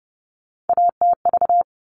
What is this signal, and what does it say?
Morse code, heard as a keyed tone, about 30 wpm, 700 Hz. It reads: ATV
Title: All Cut Numbers - 3-Digit 30wpm